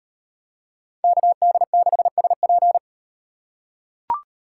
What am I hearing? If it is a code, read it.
Morse code, heard as a keyed tone, about 37 wpm, 700 Hz. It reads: KD6SP